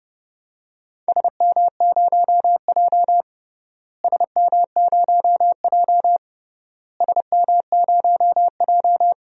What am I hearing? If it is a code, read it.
Morse code, heard as a keyed tone, about 30 wpm, 700 Hz. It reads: SM0J SM0J SM0J
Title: DXCC-100 Call Signs - 4 Characters - Repeated 3x - 30wpm